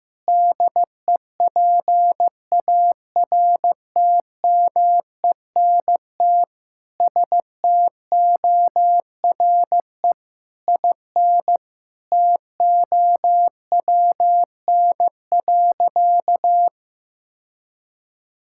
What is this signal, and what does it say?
Morse code, heard as a keyed tone, about 16 wpm, 700 Hz. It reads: DEPARTMENT STORE IN TOWN.
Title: Sentences from Top 1000 Words - Review Part 5 - 15wpm